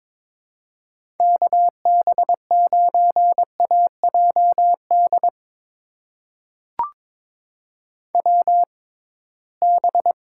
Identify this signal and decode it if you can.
Morse code, heard as a keyed tone, about 22 wpm, spaced as for 17 wpm, 700 Hz. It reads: KB9AJD WB
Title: USA Call Signs - 6 Characters - 6x Character Spacing - 22wpm